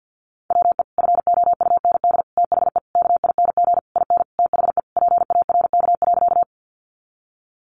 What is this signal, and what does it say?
Morse code, heard as a keyed tone, about 50 wpm, 700 Hz. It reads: WE FOUND THE KING IN THE PARK.
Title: Sentences from Top 1300 Words 50wpm